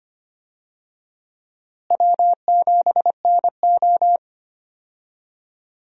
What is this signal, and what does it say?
Morse code, heard as a keyed tone, about 25 wpm, 700 Hz. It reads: W7NO
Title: USA Call Signs - 4 Characters - Repeated 2x - 25wpm